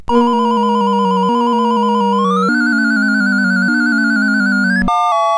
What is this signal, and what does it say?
There is no code.